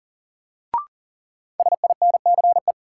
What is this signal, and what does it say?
Morse code, heard as a keyed tone, about 40 wpm, 700 Hz. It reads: SINCE
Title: Sentences from Top 300 Words - New Word Drill - 40wpm